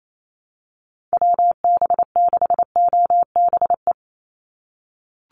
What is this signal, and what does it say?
Morse code, heard as a keyed tone, about 28 wpm, 700 Hz. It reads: WB6OBE
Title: USA Call Signs 28wpm